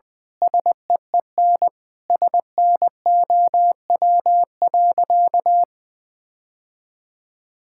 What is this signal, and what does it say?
Morse code, heard as a keyed tone, about 20 wpm, 700 Hz. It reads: SEEN SNOW.
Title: Sentences with 8-Word Phrases - Spoken Beginning - 20wpm